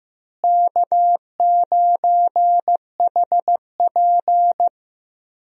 Morse code, heard as a keyed tone, about 16 wpm, 700 Hz. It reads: K9HP
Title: USA Call Signs - 4 Characters - Repeated 2x - 15wpm